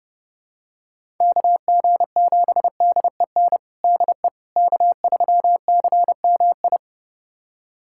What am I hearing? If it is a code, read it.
Morse code, heard as a keyed tone, about 30 wpm, 700 Hz. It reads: KG7DEN DE K3CMI
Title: USA Callsign DE Callsign 30wpm